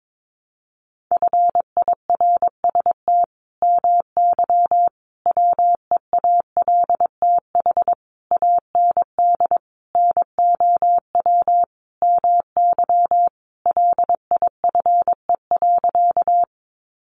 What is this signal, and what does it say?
Morse code, heard as a keyed tone, about 22 wpm, 700 Hz. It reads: FIRST MY WEALTH AND NOW MY LIFE.
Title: Sentences from Top 1200 Words 22wpm